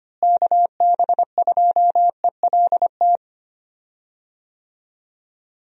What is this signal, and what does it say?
Morse code, heard as a keyed tone, about 25 wpm, 700 Hz. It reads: KB2ELT
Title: USA Call Signs - 6 Characters - Repeated 2x - 25wpm